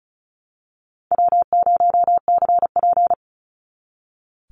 Morse code, heard as a keyed tone, about 36 wpm, 700 Hz. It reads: W0CP
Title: NA SOTA Activator Callsigns 35wpm